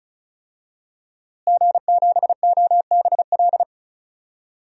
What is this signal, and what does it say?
Morse code, heard as a keyed tone, about 36 wpm, 700 Hz. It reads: G7OBL
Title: DXCC-100 Call Signs - 5 Characters - Repeated 3x - 35wpm